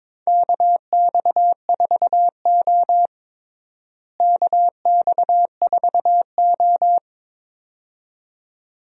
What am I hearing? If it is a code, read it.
Morse code, heard as a keyed tone, about 22 wpm, 700 Hz. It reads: KX4O KX4O
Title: USA Call Signs - 4 Characters - Repeated 2x - 22wpm